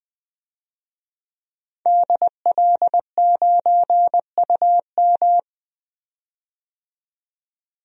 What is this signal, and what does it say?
Morse code, heard as a keyed tone, about 20 wpm, 700 Hz. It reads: DL9UM
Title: DXCC-100 Call Signs - 5 Characters - 20wpm